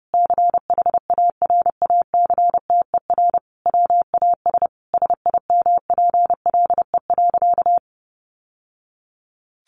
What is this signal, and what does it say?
Morse code, heard as a keyed tone, about 30 wpm, 700 Hz. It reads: CHARACTER WAS SIMPLE.
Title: Sentences from Top 1200 Words 30wpm